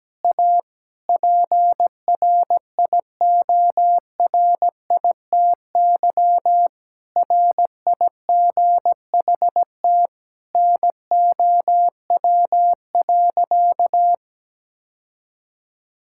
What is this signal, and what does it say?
Morse code, heard as a keyed tone, about 17 wpm, 700 Hz. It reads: A PRIORITY RIGHT NOW.
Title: Sentences from Top 1600 Words 17wpm